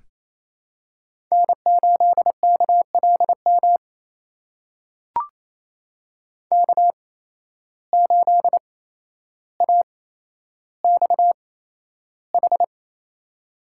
Morse code, heard as a keyed tone, about 28 wpm, spaced as for 12 wpm, 700 Hz. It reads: N8KLM K8AXH
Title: USA Call Signs - 5 Characters - 8x Character Spacing - 28wpm